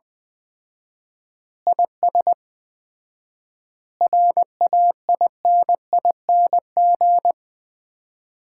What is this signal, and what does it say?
Morse code, heard as a keyed tone, about 20 wpm, 700 Hz. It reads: IS RAINING
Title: Common QSO Phrases - 4x Word-Spacing - 20wpm